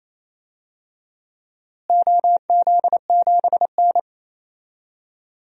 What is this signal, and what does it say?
Morse code, heard as a keyed tone, about 28 wpm, 700 Hz. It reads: OZ7N
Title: DXCC-100 Call Signs - 4 Characters - Repeated 3x - 28wpm